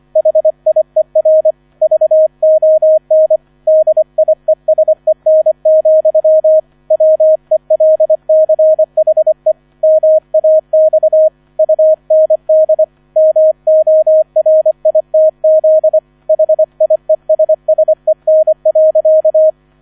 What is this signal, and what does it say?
Morse code, heard as a keyed tone, about 25 wpm, 630 Hz, with radio noise. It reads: HIER VON DIESEN, WELCHE MAX UND MORITZ HIESSEN.